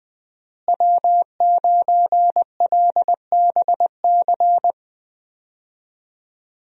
Morse code, heard as a keyed tone, about 20 wpm, 700 Hz. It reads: W9LBC
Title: USA Call Signs - 5 Characters - Repeated 3x - 20wpm